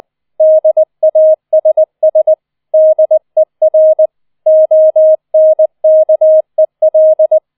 Morse code, heard as a keyed tone, about 20 wpm, 630 Hz. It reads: DASS DER ONKEL
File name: 20wpm